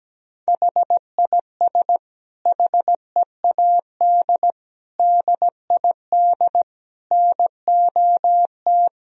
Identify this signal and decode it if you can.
Morse code, heard as a keyed tone, about 17 wpm, 700 Hz. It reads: HIS HEAD DID NOT
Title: Sentences from Top 1000 Words - Review Part 2 - 17wpm